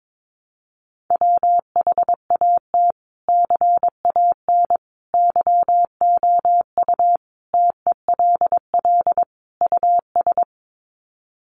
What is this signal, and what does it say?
Morse code, heard as a keyed tone, about 22 wpm, 700 Hz. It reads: WHAT CAN YOU TELL US
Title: Sets of 5 Words Encore 22wpm